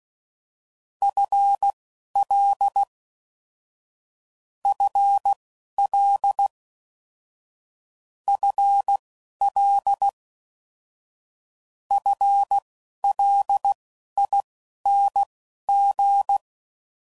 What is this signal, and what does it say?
Morse code, heard as a keyed tone, about 16 wpm, spaced as for 11 wpm, 800 Hz. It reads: FL FL FL FLING
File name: Basic Sets / QST de WA7RSO / Combnations